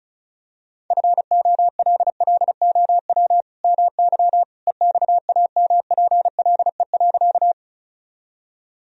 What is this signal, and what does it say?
Morse code, heard as a keyed tone, about 34 wpm, 700 Hz. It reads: FOLLOW MY EXAMPLE.
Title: Sentences from Top 1000 Words - Review Part 1 - 35wpm